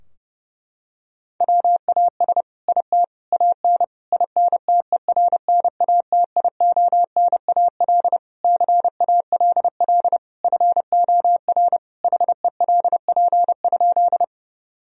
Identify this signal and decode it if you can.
Morse code, heard as a keyed tone, about 30 wpm, 700 Hz. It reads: WAS IT AN INTERNATIONAL CALL FOR HELP?
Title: Sentences from Top 1000 Words - Review Part 4 - 30wpm